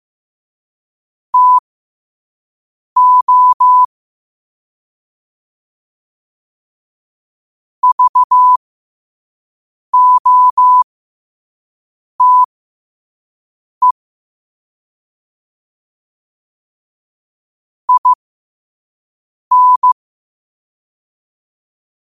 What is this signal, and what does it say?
Morse code, heard as a keyed tone, about 15 wpm, spaced as for 5 wpm, 1000 Hz. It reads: TO VOTE IN
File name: qotdfiles / 5wpm